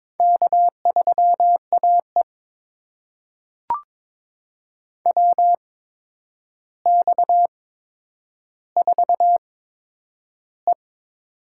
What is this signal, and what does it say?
Morse code, heard as a keyed tone, about 22 wpm, spaced as for 9 wpm, 700 Hz. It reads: K3AE WX4E
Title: USA Call Signs - 4 Characters - 8x Character Spacing - 22wpm